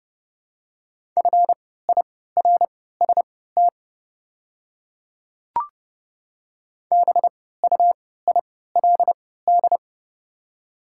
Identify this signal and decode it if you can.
Morse code, heard as a keyed tone, about 29 wpm, spaced as for 17 wpm, 700 Hz. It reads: FIRST BUILD
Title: 3x Character Spacing - Top 200 Words - 30wpm